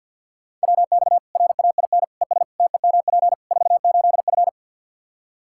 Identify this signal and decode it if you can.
Morse code, heard as a keyed tone, about 52 wpm, 700 Hz. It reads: WX RAIN ES TEMP 48F
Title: Practice QSO - Exchange 2 - Info - 50wpm